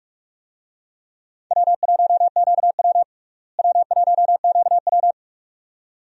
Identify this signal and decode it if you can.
Morse code, heard as a keyed tone, about 46 wpm, 700 Hz. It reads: W1QW W1QW
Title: USA Call Signs - 4 Characters - Repeated 2x - 45wpm